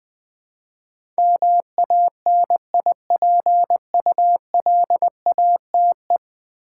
Morse code, heard as a keyed tone, about 20 wpm, 700 Hz. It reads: MANIPULATE
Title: Build A Word - 4 Letters At A Time - 10-Letter Words - 20wpm